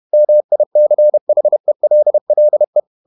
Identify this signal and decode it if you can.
Morse code, heard as a keyed tone, about 31 wpm, 600 Hz. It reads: MICHELLE
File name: Ferie QRQ 31wpm